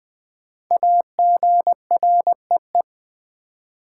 Morse code, heard as a keyed tone, about 20 wpm, 700 Hz. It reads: AGREE
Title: Sentences from Top 1100 Words - New Word Drill Alt - 20wpm